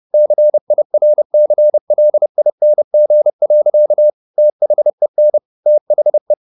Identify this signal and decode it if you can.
Morse code, heard as a keyed tone, about 30 wpm, 600 Hz. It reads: CIRCLING. THEN THE